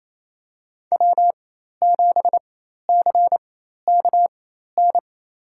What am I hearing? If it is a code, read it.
Morse code, heard as a keyed tone, about 28 wpm, spaced as for 13 wpm, 700 Hz. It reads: W7CKN
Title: USA Call Signs - 5 Characters - 4x Character Spacing - 28wpm